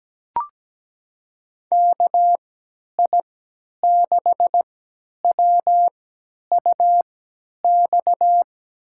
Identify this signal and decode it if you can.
Morse code, heard as a keyed tone, about 17 wpm, spaced as for 10 wpm, 700 Hz. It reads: KI6WUX
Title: USA Call Signs - 3x Character Spacing - 17wpm